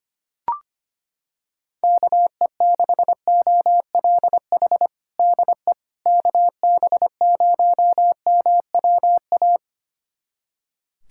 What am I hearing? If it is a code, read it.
Morse code, heard as a keyed tone, about 25 wpm, 700 Hz. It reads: KE6OLH DE KB0MWA
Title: USA Callsign DE Callsign - 6 Characters - 25wpm